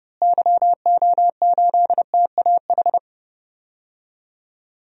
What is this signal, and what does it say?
Morse code, heard as a keyed tone, about 32 wpm, 700 Hz. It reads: YO8TAH
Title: DXCC-100 Call Signs - 6 Characters - Repeated 2x - 30wpm